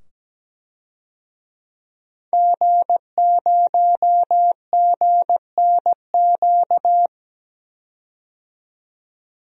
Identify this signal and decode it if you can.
Morse code, heard as a keyed tone, about 17 wpm, 700 Hz. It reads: G0GNQ
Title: DXCC-100 Call Signs - 5 Characters - 17wpm